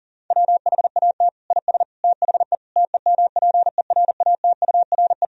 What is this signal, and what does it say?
Morse code, heard as a keyed tone, about 40 wpm, 700 Hz. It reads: WHAT IS THE TEMPERATURE
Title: Sentences from Top 1000 Words - Review Part 4 - 40wpm